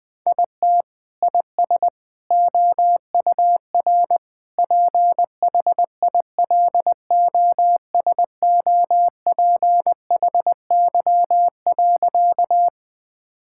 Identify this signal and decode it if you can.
Morse code, heard as a keyed tone, about 20 wpm, 700 Hz. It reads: IT IS OUR PHILOSOPHY.
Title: Sentences from Top 1000 Words - Review Part 6 - 20wpm